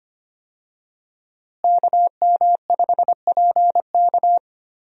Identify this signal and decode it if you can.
Morse code, heard as a keyed tone, about 25 wpm, 700 Hz. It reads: KM5PK
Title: USA Call Signs - 5 Characters - Repeated 3x - 25wpm